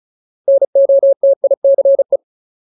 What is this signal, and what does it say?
Morse code, heard as a keyed tone, about 34 wpm, 550 Hz. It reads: NOTICE
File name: Ultimate Word List / 35WPM Ultimate